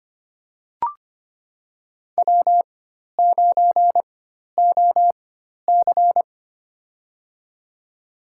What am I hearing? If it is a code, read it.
Morse code, heard as a keyed tone, about 25 wpm, spaced as for 12 wpm, 700 Hz. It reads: W9OC
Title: USA Call Signs - 4 Characters - 4x Character Spacing - 25wpm